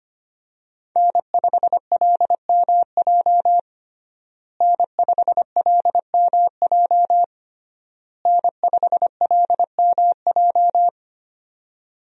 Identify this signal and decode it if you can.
Morse code, heard as a keyed tone, about 25 wpm, 700 Hz. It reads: N5LMJ N5LMJ N5LMJ
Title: USA Call Signs - 5 Characters - Repeated 3x - 25wpm